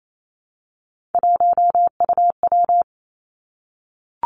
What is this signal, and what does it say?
Morse code, heard as a keyed tone, about 28 wpm, 700 Hz. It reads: 1UW
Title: Three Challenging Letter-Number Combinations 28wpm